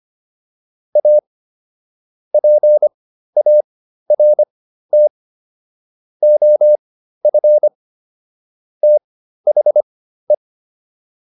Morse code, heard as a keyed tone, about 25 wpm, spaced as for 13 wpm, 600 Hz. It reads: A PART OF THE
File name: WD302 13